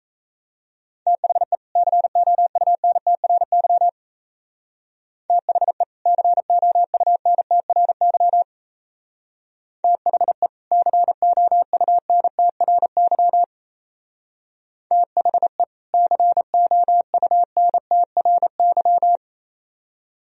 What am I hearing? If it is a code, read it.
Morse code, heard as a keyed tone, about 33 wpm, 690 Hz. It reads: THE COUNTRY THE COUNTRY THE COUNTRY THE COUNTRY
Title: Speed-Racing - Sets of 2 Words 28wpm